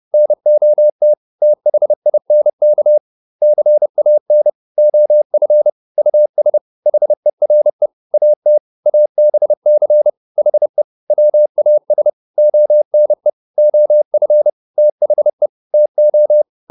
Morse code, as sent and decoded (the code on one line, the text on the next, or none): -. --- - / - .... .. -. -.- / -.-. .- -. / --- ..-. / ..- ... / .... .-. / .- - / .- -... -.-. / .... / .-- .- ... / --- -. / --- ..-. / - .... / - ---
NOT THINK CAN OF US HERE AT ABC HE WAS ONE OF THE TO